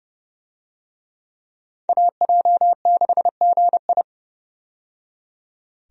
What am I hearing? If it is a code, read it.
Morse code, heard as a keyed tone, about 30 wpm, 700 Hz. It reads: AJ6GI
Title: USA Call Signs - 5 Characters - 30wpm